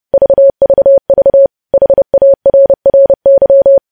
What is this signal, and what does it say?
Morse code, heard as a keyed tone, about 30 wpm, 550 Hz. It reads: VVV HARRY
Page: news headlines in morse code at 30 wpm